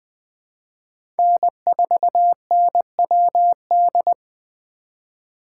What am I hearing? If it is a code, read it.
Morse code, heard as a keyed tone, about 19 wpm, 700 Hz. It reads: N4NWD